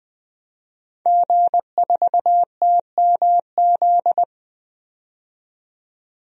Morse code, heard as a keyed tone, about 20 wpm, 700 Hz. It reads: G4TMZ
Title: DXCC-100 Call Signs - 5 Characters - 20wpm